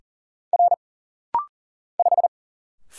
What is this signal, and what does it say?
Morse code, heard as a keyed tone, about 40 wpm, 700 Hz. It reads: R 5
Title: Single Letter-Number - Rapid-Fire 40wpm